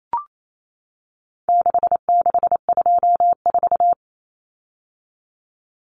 Morse code, heard as a keyed tone, about 28 wpm, 700 Hz. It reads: 6624
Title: Numbers - 4-Digit 28wpm